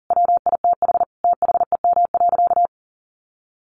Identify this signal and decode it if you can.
Morse code, heard as a keyed tone, about 40 wpm, 700 Hz. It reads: WITH THEM.
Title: Sentences from Top 700 Words 40wpm